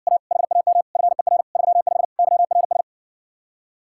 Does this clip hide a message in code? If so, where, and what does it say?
Yes: Morse code from 0.1 to 2.8 s, heard as a keyed tone, about 60 wpm, 700 Hz. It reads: A HAM FER 35 YRS